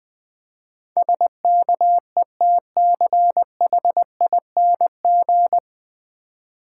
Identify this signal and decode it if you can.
Morse code, heard as a keyed tone, about 20 wpm, 700 Hz. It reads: SKETCHING
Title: Easy Words - Speed-Racing with Hints - 20wpm